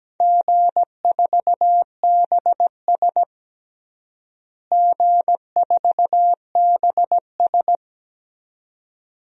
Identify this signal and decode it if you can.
Morse code, heard as a keyed tone, about 17 wpm, 700 Hz. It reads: G4BS G4BS
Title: DXCC-100 Call Signs - 4 Characters - Repeated 2x - 17wpm